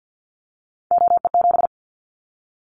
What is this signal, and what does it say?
Morse code, heard as a keyed tone, about 50 wpm, 700 Hz. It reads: OE7
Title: Callsign Prefixes for Top 100 DXCC - 50wpm